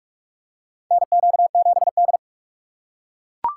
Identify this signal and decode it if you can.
Morse code, heard as a keyed tone, about 42 wpm, 700 Hz. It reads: NQ7D
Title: USA Call Signs - 4 Characters - 2x Character Spacing - 45wpm